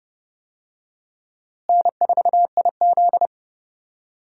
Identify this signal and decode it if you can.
Morse code, heard as a keyed tone, about 30 wpm, 700 Hz. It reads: N4IZ
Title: USA Call Signs - 4 Characters - Repeated 2x - 30wpm